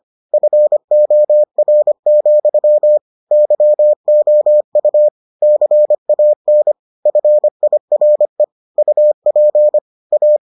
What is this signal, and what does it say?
Morse code, heard as a keyed tone, about 25 wpm, 600 Hz. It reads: FOR, YOU CAN FIRE UP A